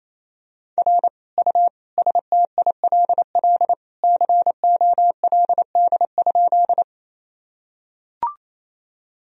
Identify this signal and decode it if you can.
Morse code, heard as a keyed tone, about 28 wpm, 700 Hz. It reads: R U STILL COLD?
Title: Sentences from Top 400 Words and Abbreviations 28wpm